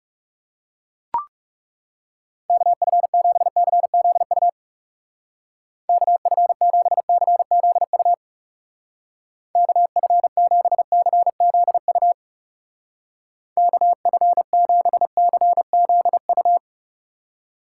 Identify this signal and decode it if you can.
Morse code, heard as a keyed tone, about 38 wpm, 700 Hz. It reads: KF7CZU KF7CZU KF7CZU KF7CZU